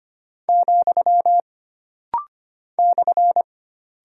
Morse code, heard as a keyed tone, about 25 wpm, 700 Hz. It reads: , /
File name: Single Punctuation-Prosign - Rapid-Fire - 25wpm